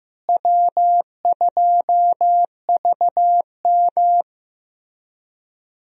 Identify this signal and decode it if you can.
Morse code, heard as a keyed tone, about 15 wpm, 700 Hz. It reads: W2VM
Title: USA Call Signs - 4 Characters - Repeated 3x - 15wpm